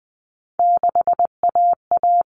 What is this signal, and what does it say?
Morse code, heard as a keyed tone, about 20 wpm, 700 Hz. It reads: 6AA
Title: Common Cut Numbers - 3-Digit 20wpm